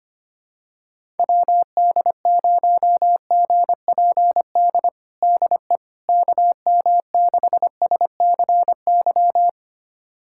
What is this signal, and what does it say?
Morse code, heard as a keyed tone, about 25 wpm, 700 Hz. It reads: WD0GPD DE KM6SCY
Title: USA Callsign DE Callsign 25wpm